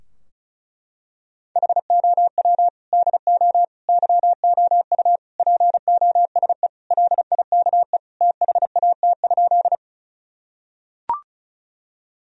Morse code, heard as a keyed tone, about 34 wpm, 700 Hz. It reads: HOW DO YOU POSE LIKE THAT?
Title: Sentences from Top 1000 Words - Review Part 2 - 35wpm